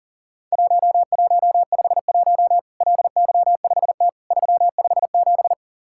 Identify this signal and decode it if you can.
Morse code, heard as a keyed tone, about 42 wpm, 700 Hz. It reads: 1151 LY5T 357
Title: Worked All Europe DX Contest - QTC Exchange - 40wpm